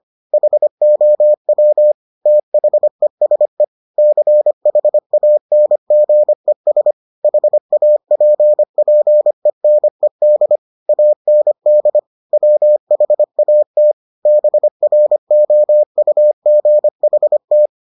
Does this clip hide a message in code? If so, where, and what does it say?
Yes: Morse code from 0.3 to 17.7 s, heard as a keyed tone, about 25 wpm, 600 Hz. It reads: HOW THESE CHANGES HAPPENED AND WHAT BROUGHT